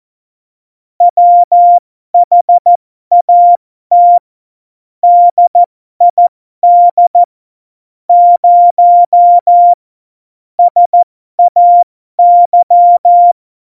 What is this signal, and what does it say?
Morse code, heard as a keyed tone, about 14 wpm, spaced as for 12 wpm, 700 Hz. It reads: WHAT DID 0 SAY
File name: Morsekurs / Witze